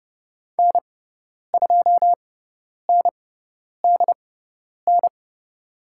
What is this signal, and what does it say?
Morse code, heard as a keyed tone, about 30 wpm, spaced as for 10 wpm, 690 Hz. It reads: N2NDN
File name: Lesson 18g - Letter m - Call Signs 30wpm